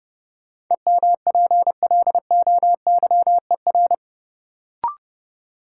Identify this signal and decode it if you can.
Morse code, heard as a keyed tone, about 30 wpm, 700 Hz. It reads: EMPLOYER
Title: Sentences from Top 1600 Words - New Word Drill - 30wpm